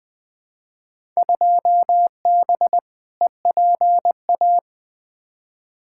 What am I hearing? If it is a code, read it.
Morse code, heard as a keyed tone, about 20 wpm, 700 Hz. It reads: 2B EPA
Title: Field Day Exchanges Encore - 20wpm